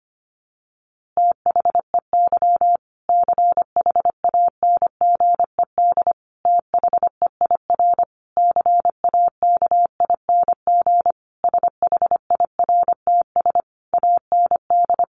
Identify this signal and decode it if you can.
Morse code, heard as a keyed tone, about 25 wpm, 700 Hz. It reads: THEY CHANGED THEIR CAKING SHIRTS AND
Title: A Sound Of Thunder 25wpm